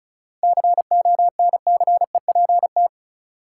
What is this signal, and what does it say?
Morse code, heard as a keyed tone, about 34 wpm, 700 Hz. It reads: CONCEPT
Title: Seven Letter Words - 35wpm